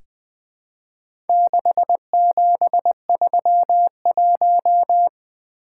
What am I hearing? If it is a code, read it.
Morse code, heard as a keyed tone, about 20 wpm, 700 Hz. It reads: 6731
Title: Four Letter-Number Combinations - 20wpm